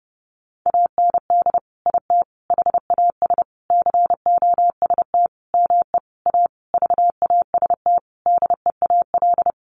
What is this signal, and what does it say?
Morse code, heard as a keyed tone, about 28 wpm, 700 Hz. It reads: AND IT HAS COST ME A VAST DEAL
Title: Sentences from Top 1300 Words 30wpm